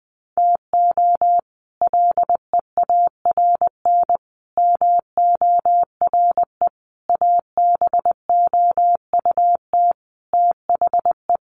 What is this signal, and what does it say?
Morse code, heard as a keyed tone, about 20 wpm, 700 Hz. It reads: TO LEARN MORE ABOUT THE
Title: Sets of 5 Words Encore2 20wpm